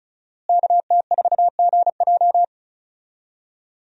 Morse code, heard as a keyed tone, about 34 wpm, 700 Hz. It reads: KT4GJ
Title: USA Call Signs - 5 Characters - Repeated 2x - 35wpm